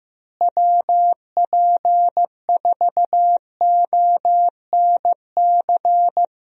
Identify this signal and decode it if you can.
Morse code, heard as a keyed tone, about 15 wpm, 700 Hz. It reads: WP4ONC
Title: USA Call Signs - 6 Characters - Repeated 3x - 15wpm